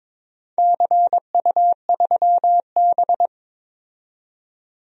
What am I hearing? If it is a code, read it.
Morse code, heard as a keyed tone, about 22 wpm, 700 Hz. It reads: CU3B
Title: DXCC-100 Call Signs - 4 Characters - 22wpm